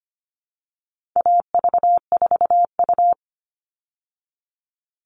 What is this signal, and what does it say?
Morse code, heard as a keyed tone, about 25 wpm, 700 Hz. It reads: AV4U
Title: All Cut Numbers - 4-Digit 25wpm